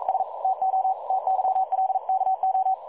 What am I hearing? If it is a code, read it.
Morse code, heard as a keyed tone, about 42 wpm, 760 Hz, with radio noise. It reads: S TO TYLKO
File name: dlazaawansowanych / nagraniazpasma / websdr recording 3565.0kHz